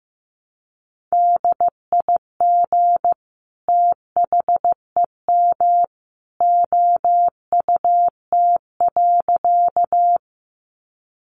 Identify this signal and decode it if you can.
Morse code, heard as a keyed tone, about 15 wpm, 700 Hz. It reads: DIG THEM OUT.
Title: A Sound Of Thunder 15wpm